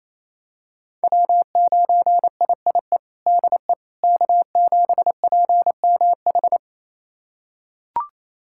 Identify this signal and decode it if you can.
Morse code, heard as a keyed tone, about 28 wpm, 700 Hz. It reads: W9IIE DE K7PMH
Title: USA Callsign DE Callsign - 5 Characters - 28wpm